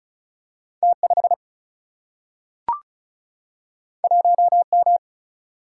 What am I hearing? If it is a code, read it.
Morse code, heard as a keyed tone, about 36 wpm, 700 Hz. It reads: T5 1M
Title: Single-Letter Single-Number 35wpm